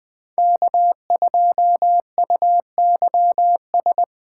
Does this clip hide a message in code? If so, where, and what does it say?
Yes: Morse code from 0.4 to 4.0 s, heard as a keyed tone, about 21 wpm, 700 Hz. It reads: K2UYS